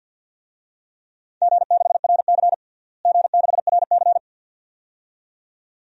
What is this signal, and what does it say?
Morse code, heard as a keyed tone, about 46 wpm, 700 Hz. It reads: G6RC G6RC
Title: DXCC-100 Call Signs - 4 Characters - Repeated 2x - 50wpm